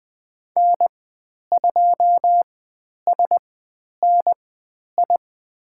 Morse code, heard as a keyed tone, about 20 wpm, spaced as for 10 wpm, 700 Hz. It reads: N2SNI
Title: 109 Lesson 18g - Letter m - Call Signs 20wpm